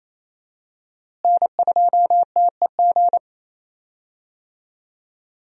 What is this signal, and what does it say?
Morse code, heard as a keyed tone, about 27 wpm, 700 Hz. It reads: N2TEG
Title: USA Call Signs - 5 Characters - Repeated 2x - 28wpm